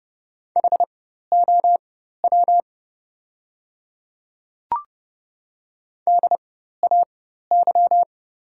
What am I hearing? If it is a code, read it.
Morse code, heard as a keyed tone, about 30 wpm, spaced as for 14 wpm, 700 Hz. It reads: HOW DAY
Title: 4x Character Spacing - Top 100 Words - 30wpm